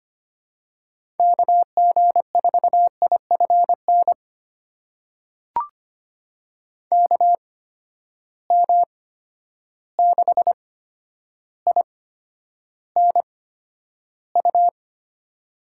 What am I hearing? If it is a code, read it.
Morse code, heard as a keyed tone, about 25 wpm, spaced as for 11 wpm, 700 Hz. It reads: KG4IFN KM6INU
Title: USA Call Signs - 6 Characters - 8x Character Spacing - 25wpm